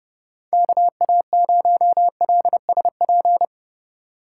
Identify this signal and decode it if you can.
Morse code, heard as a keyed tone, about 30 wpm, 700 Hz. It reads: KA0LSP
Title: USA Call Signs - 6 Characters - 2x Character Spacing - 30wpm